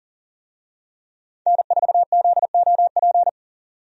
Morse code, heard as a keyed tone, about 40 wpm, 700 Hz. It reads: N4ZOP
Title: USA Call Signs - 5 Characters - 40wpm